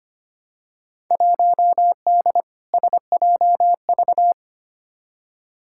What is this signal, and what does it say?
Morse code, heard as a keyed tone, about 25 wpm, 700 Hz. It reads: 1D SJV
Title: Field Day Exchanges Encore - 25wpm